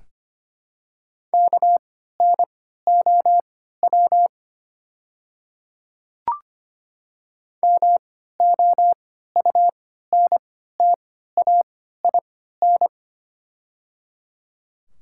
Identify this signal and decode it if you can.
Morse code, heard as a keyed tone, about 25 wpm, spaced as for 14 wpm, 700 Hz. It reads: KNOW MOUNTAIN
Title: 3x Character Spacing - Top 300 Words - 25wpm